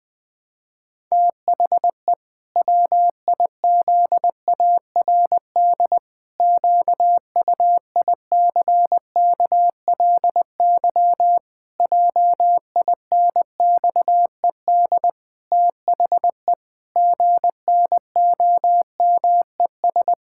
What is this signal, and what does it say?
Morse code, heard as a keyed tone, about 20 wpm, 700 Hz. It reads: THE WIZARD QUICKLY JINXED THE GNOMES
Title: Pangram Challenge v2 - 20wpm